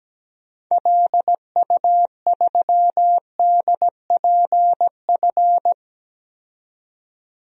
Morse code, heard as a keyed tone, about 17 wpm, 700 Hz. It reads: LU3DPF
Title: DXCC-100 Call Signs - 6 Characters - 17wpm